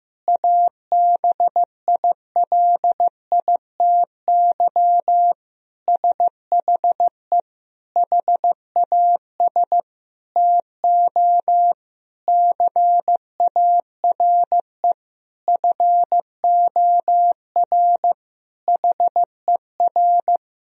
Morse code, heard as a keyed tone, about 16 wpm, 700 Hz. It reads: ABILITY SHE HAS TO CARE FOR HER
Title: Sentences from Top 1000 Words - Review Part 3 - 15wpm